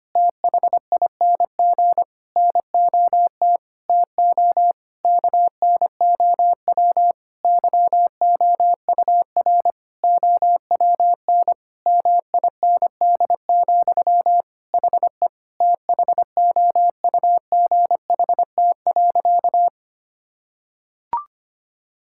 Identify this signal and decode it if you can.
Morse code, heard as a keyed tone, about 25 wpm, 700 Hz. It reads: THING NOT TO KNOW YOUR OWN MIND, HE THOUGHT.
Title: Sentences from Top 1000 Words - Review Part 6 - 25wpm